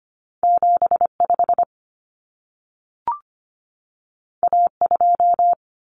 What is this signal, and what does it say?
Morse code, heard as a keyed tone, about 25 wpm, 700 Hz. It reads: 75 A2
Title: Common Cut Numbers - 2-Digit 25wpm